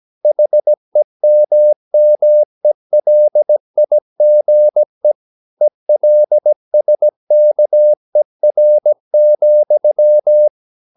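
Morse code, heard as a keyed tone, about 17 wpm, 600 Hz. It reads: HEMMELIGE ELSKER,
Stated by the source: Slow (QRS) Morse practice at 17 wpm, from CW med Gnister.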